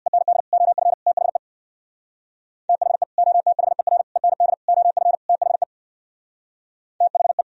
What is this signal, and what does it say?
Morse code, heard as a keyed tone, about 61 wpm, 700 Hz. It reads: END OF THE THE OTHER END OF THE THE
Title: Speed-Racing - Sets of 5 Words 50wpm